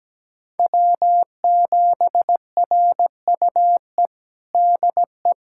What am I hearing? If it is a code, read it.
Morse code, heard as a keyed tone, about 17 wpm, 700 Hz. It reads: W7RUE DE
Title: USA Callsign DE Callsign - 5 Characters - 17wpm